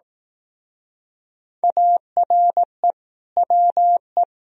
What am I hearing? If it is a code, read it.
Morse code, heard as a keyed tone, about 18 wpm, 700 Hz. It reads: ARE WE
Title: Speed-Racing - Sets of 4 Words 15wpm